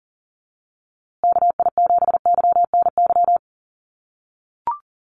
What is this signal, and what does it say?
Morse code, heard as a keyed tone, about 40 wpm, 700 Hz. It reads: KI7YNY
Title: USA Call Signs Repeated 3x 40wpm